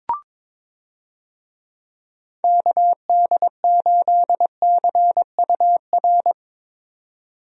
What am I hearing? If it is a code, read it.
Morse code, heard as a keyed tone, about 22 wpm, 700 Hz. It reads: KD8CUR
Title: USA Call Signs - 6 Characters - 22wpm